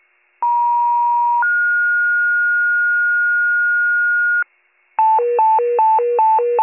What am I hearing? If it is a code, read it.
Two-tone page, heard as a one-second tone, then a three-second tone, about 950 then 1430 Hz, with noise. It is heard from 0.4 to 4.4 s.